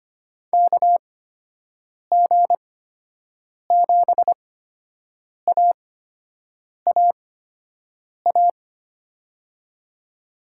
Morse code, heard as a keyed tone, about 25 wpm, spaced as for 7 wpm, 700 Hz. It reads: KG7AAA
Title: USA Call Signs - 6 Characters - 8x Character Spacing - 25wpm